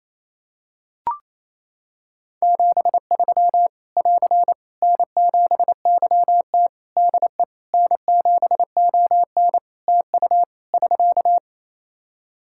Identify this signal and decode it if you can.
Morse code, heard as a keyed tone, about 27 wpm, 700 Hz. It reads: 73 <AR> N7YT DE N7ON TU <SK>
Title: Practice QSO - Exchange 4 - Ending - 28wpm